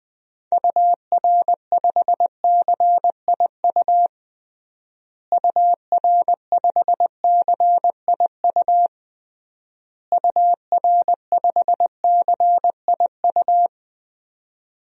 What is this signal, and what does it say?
Morse code, heard as a keyed tone, about 20 wpm, 700 Hz. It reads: UR5CIU UR5CIU UR5CIU
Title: DXCC-100 Call Signs - 6 Characters - Repeated 3x - 20wpm